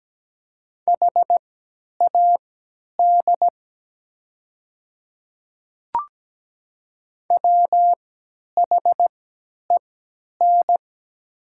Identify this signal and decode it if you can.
Morse code, heard as a keyed tone, about 17 wpm, spaced as for 10 wpm, 690 Hz. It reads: HAD WHEN
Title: 3x Character Spacing - Top 100 Words - 17wpm